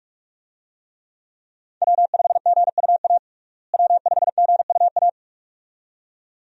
Morse code, heard as a keyed tone, about 42 wpm, 700 Hz. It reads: W5GUA W5GUA
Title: USA Call Signs - 5 Characters - Repeated 2x - 45wpm